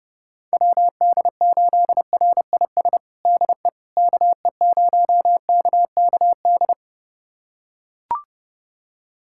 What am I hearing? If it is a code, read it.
Morse code, heard as a keyed tone, about 30 wpm, 700 Hz. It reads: WD8RIS DE KE0KKD
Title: USA Callsign DE Callsign - 6 Characters - 30wpm